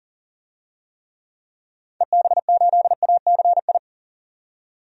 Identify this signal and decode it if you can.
Morse code, heard as a keyed tone, about 40 wpm, 700 Hz. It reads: EB8ACI
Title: DXCC-100 Call Signs - 6 Characters - 40wpm